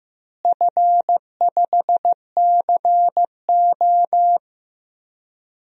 Morse code, heard as a keyed tone, about 15 wpm, 700 Hz. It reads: F5CO
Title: DXCC-100 Call Signs - 4 Characters - Repeated 2x - 15wpm